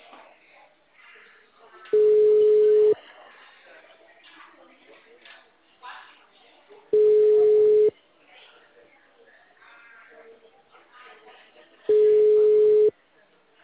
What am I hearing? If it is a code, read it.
Ringback tone (Europe, 425 Hz), with speech.